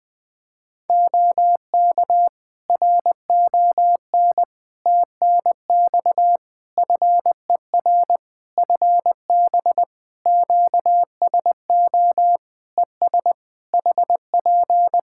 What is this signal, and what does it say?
Morse code, heard as a keyed tone, about 20 wpm, 700 Hz. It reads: OK RON TNX FER FB QSO ES HP
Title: Practice QSO - Exchange 4 - Ending - 20wpm